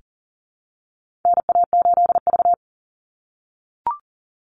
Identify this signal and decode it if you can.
Morse code, heard as a keyed tone, about 40 wpm, 700 Hz. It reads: NA8V
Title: Call Signs - 40wpm